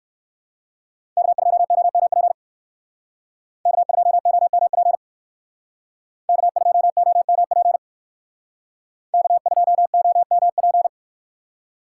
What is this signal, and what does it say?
Morse code, heard as a keyed tone, about 55 wpm, 700 Hz. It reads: K2OMP K2OMP K2OMP K2OMP
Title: Speed Racing - USA 1x3 Call Signs 45wpm